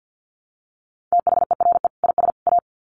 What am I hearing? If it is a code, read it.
Morse code, heard as a keyed tone, about 50 wpm, 700 Hz. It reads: THERE IS A
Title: Sets of 3 Words 50wpm